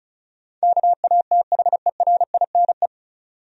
Morse code, heard as a keyed tone, about 34 wpm, 700 Hz. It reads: KATHERINE
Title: USA Common Names - 35wpm